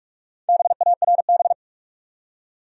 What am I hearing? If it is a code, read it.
Morse code, heard as a keyed tone, about 46 wpm, 700 Hz. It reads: BARB
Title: USA Common Names - Max 5 Letters - 45wpm